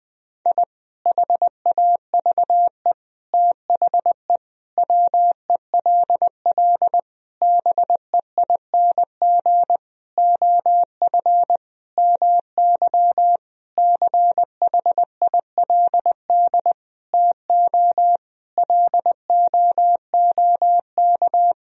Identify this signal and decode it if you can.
Morse code, heard as a keyed tone, about 20 wpm, 700 Hz. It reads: I HAVE THE WELL BEING OF MY CHILD TO LOOK
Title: Sentences from Top 1000 Words - Review Part 3 - 20wpm